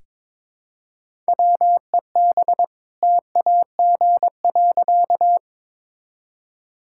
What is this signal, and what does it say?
Morse code, heard as a keyed tone, about 22 wpm, 690 Hz. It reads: WEB TAG.
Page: morse-code-ninja-practice-22wpm